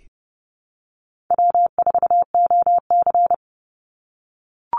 Morse code, heard as a keyed tone, about 30 wpm, 700 Hz. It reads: W4OC
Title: Easy Call Signs - 30wpm